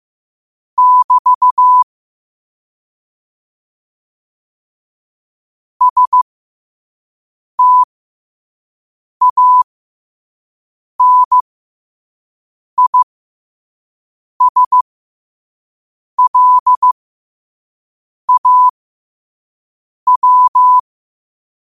Morse code, heard as a keyed tone, about 15 wpm, spaced as for 5 wpm, 1000 Hz. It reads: <BT> STANISLAW
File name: qotdfiles / 5wpm